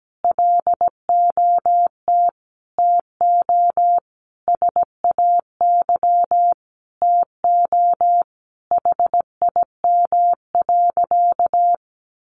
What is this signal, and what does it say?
Morse code, heard as a keyed tone, about 17 wpm, 700 Hz. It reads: LOT TO SAY TO HIM.
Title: Sentences from Top 1000 Words 17wpm